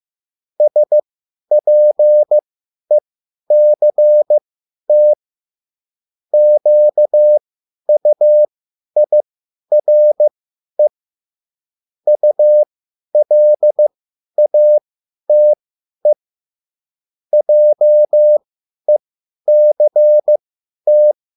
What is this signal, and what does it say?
Morse code, heard as a keyed tone, about 15 wpm, spaced as for 10 wpm, 600 Hz. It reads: SPECT QUIRE ULATE JECT